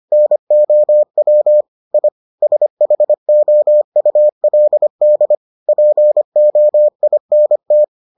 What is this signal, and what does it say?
Morse code, heard as a keyed tone, about 25 wpm, 600 Hz. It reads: NOW I SHOULD POINT